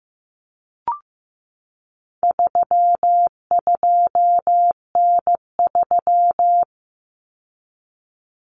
Morse code, heard as a keyed tone, about 15 wpm, 700 Hz. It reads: 32N3
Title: Common Cut Numbers - 4-Digit 15wpm